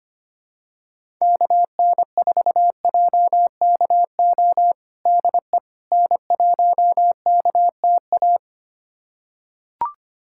Morse code, heard as a keyed tone, about 25 wpm, 700 Hz. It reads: KN4JKO DE N1KTA
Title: USA Callsign DE Callsign 25wpm